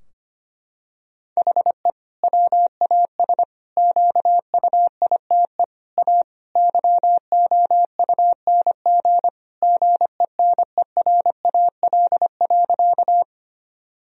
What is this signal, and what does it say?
Morse code, heard as a keyed tone, about 25 wpm, 700 Hz. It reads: HE WAS QUITE A YOUNG GENERAL.
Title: Sentences from Top 1000 Words - Review Part 3 - 25wpm